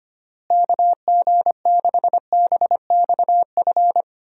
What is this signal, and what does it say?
Morse code, heard as a keyed tone, about 25 wpm, 700 Hz. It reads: KG6BXF
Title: USA Call Signs - 6 Characters - Repeated 3x - 25wpm